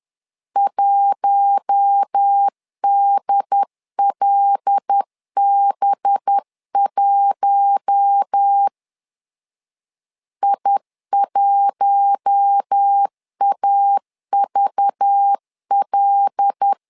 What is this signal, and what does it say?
Morse code, heard as a keyed tone, about 10 wpm, 800 Hz. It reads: 1DLB1 I1AVL
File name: LEZIONI / 5 LBVC / 5 LBV 10wpm 2248